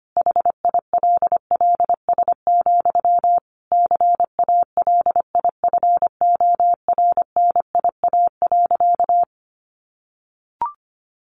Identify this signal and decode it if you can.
Morse code, heard as a keyed tone, about 25 wpm, 700 Hz. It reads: HILLS, CALIFORNIA.
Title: Easy Phrases 25wpm